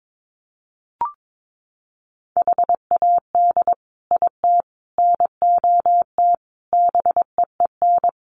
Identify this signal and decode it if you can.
Morse code, heard as a keyed tone, about 22 wpm, 700 Hz. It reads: HAD IT NOT BEEN
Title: Sets of 5 Words Encore2 22wpm